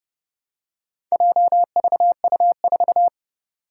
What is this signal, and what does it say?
Morse code, heard as a keyed tone, about 30 wpm, 700 Hz. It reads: JVU4
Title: Four Challenging Letter-Number Combinations 30wpm